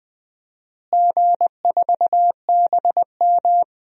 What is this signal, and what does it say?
Morse code, heard as a keyed tone, about 21 wpm, 700 Hz. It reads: G4BM
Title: DXCC-100 Call Signs - 4 Characters - Repeated 2x - 20wpm